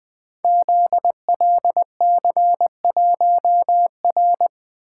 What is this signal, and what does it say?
Morse code, heard as a keyed tone, about 20 wpm, 700 Hz. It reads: ZLC1R
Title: Code Groups - 5 Letter:Numbers - 20wpm